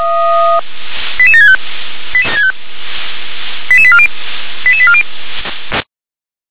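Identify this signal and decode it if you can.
Five-tone selective call (ZVEI), heard as quick runs of five tones, tones of about 70 ms. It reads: 88765 88840 88840
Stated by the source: ZVEI five-tone selective call received by SDR.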